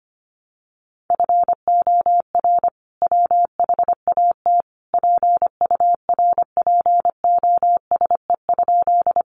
Morse code, heard as a keyed tone, about 25 wpm, 700 Hz. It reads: FOR WHAT PURPOSE?
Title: Sentences from Top 800 Words 25wpm